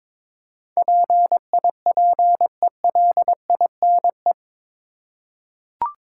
Morse code, heard as a keyed tone, about 22 wpm, 700 Hz. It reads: PIPELINE
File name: practice / 22wpm / Eight Letter Words - 22wpm